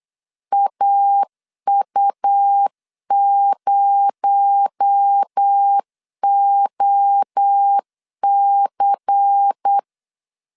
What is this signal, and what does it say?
Morse code, heard as a keyed tone, about 8 wpm, 800 Hz. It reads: AU0OC
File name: LEZIONI / 8 YJ45 / 8 YJ 8wpm 3313